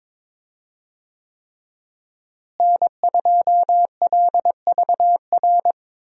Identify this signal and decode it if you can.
Morse code, heard as a keyed tone, about 22 wpm, 700 Hz. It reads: N2LVR
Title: USA Call Signs - 5 Characters - Repeated 3x - 22wpm